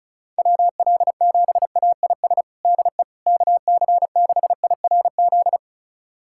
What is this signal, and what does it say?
Morse code, heard as a keyed tone, about 36 wpm, 700 Hz. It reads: WL7AIS DE KC6IRZ
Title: USA Callsign DE Callsign 35wpm